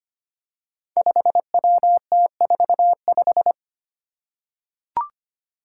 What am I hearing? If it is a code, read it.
Morse code, heard as a keyed tone, about 25 wpm, 700 Hz. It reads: 5WT45
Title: Five Challenging Letter-Number Combinations 25wpm